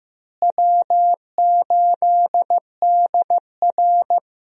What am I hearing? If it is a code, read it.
Morse code, heard as a keyed tone, about 16 wpm, 700 Hz. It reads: W8DR
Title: USA Call Signs - 4 Characters - Repeated 2x - 15wpm